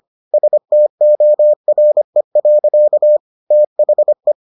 Morse code, heard as a keyed tone, about 25 wpm, 600 Hz. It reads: STORE. THE